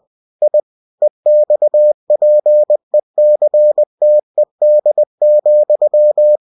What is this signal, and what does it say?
Morse code, heard as a keyed tone, about 20 wpm, 600 Hz. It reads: I EXPECTED,
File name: cw-books / winnie-the-pooh / winnie-20wpm-page-0088